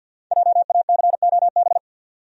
Morse code, heard as a keyed tone, about 49 wpm, 700 Hz. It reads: JACOB
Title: USA Common Names - Max 5 Letters - 50wpm